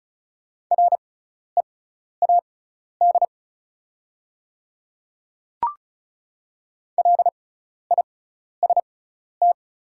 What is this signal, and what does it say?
Morse code, heard as a keyed tone, about 34 wpm, spaced as for 12 wpm, 700 Hz. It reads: READ LIST
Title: 6x Character Spacing - Top 500 Words - 35wpm